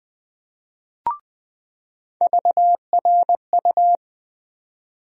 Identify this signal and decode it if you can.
Morse code, heard as a keyed tone, about 20 wpm, 700 Hz. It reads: VRU
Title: USA Call Sign Postfixes - 3 Letters 20wpm